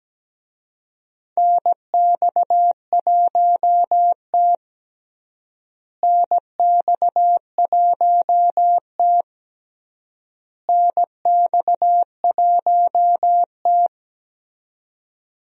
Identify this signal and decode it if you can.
Morse code, heard as a keyed tone, about 17 wpm, 700 Hz. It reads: NX1T NX1T NX1T
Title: USA Call Signs - 4 Characters - Repeated 3x - 17wpm